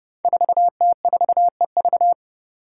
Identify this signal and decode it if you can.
Morse code, heard as a keyed tone, about 30 wpm, 700 Hz. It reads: 4T4EV
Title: Five Challenging Letter-Number Combinations 30wpm